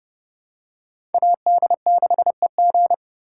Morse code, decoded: AD6EG

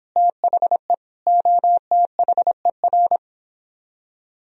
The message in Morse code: THE OTHER

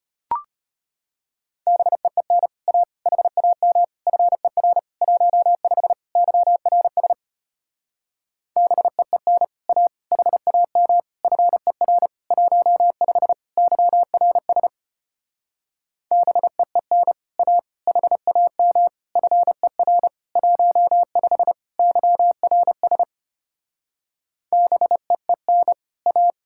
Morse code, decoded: BEEN A HAM FER 15 YRS BEEN A HAM FER 15 YRS BEEN A HAM FER 15 YRS BEEN A

